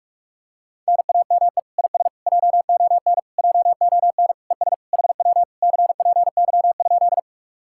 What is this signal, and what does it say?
Morse code, heard as a keyed tone, about 49 wpm, 700 Hz. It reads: NAME IS JON JON ES HW CPY?